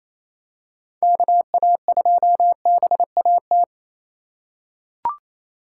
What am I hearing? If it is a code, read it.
Morse code, heard as a keyed tone, about 28 wpm, 700 Hz. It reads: KA2BAT
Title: USA Call Signs - 6 Characters - 4x Character Spacing - 28wpm